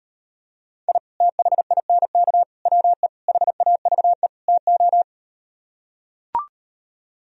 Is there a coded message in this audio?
Yes: Morse code from 0.9 to 5.0 s, heard as a keyed tone, about 38 wpm, 700 Hz. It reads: I THINK WE HAVE TO